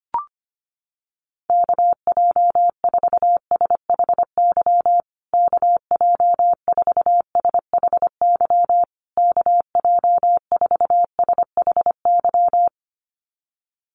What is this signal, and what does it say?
Morse code, heard as a keyed tone, about 25 wpm, 700 Hz. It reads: KJ4SHY KJ4SHY KJ4SHY
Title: USA Call Signs Repeated 3x 25wpm